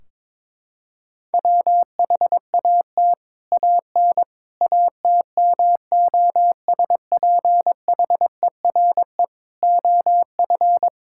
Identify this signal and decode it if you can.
Morse code, heard as a keyed tone, about 22 wpm, 700 Hz. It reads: WHAT AN ATMOSPHERE OF
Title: Sentences from Top 1600 Words 22wpm